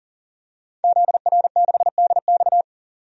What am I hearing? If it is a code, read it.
Morse code, heard as a keyed tone, about 40 wpm, 700 Hz. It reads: ZR6DX
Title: DXCC-100 Call Signs - 5 Characters - 40wpm